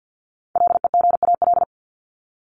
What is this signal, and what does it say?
Morse code, heard as a keyed tone, about 50 wpm, 700 Hz. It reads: LEGAL
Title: Top 1001-1250 Words 50wpm